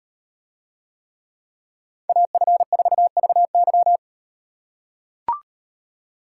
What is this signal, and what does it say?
Morse code, heard as a keyed tone, about 36 wpm, 700 Hz. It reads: AF4VY